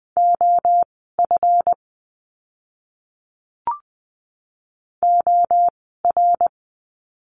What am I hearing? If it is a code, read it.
Morse code, heard as a keyed tone, about 20 wpm, spaced as for 15 wpm, 700 Hz. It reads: OF OR
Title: 192 Lesson 31f - Letter v - Words 20wpm